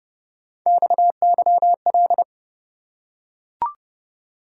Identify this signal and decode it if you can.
Morse code, heard as a keyed tone, about 30 wpm, 700 Hz. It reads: XYL
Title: Top 90 QSO Elements 30wpm